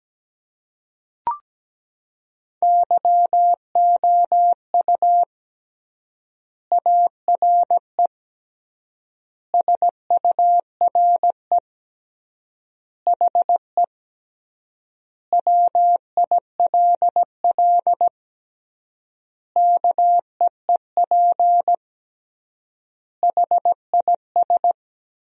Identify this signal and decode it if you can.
Morse code, heard as a keyed tone, about 17 wpm, 700 Hz. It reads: YOU ARE SURE HE WILL KEEP HIS
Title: 3x Word-Spacing - Sentences from Top 300 Words - 17wpm